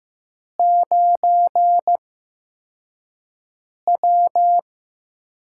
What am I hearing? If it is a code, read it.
Morse code, heard as a keyed tone, about 15 wpm, spaced as for 4 wpm, 700 Hz. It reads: 9W
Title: USA Call Signs - 4 Characters - 8x Character Spacing - 15wpm